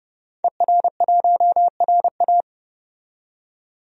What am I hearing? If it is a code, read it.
Morse code, heard as a keyed tone, about 30 wpm, 700 Hz. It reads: ER1RA
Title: DXCC-100 Call Signs - 5 Characters - 30wpm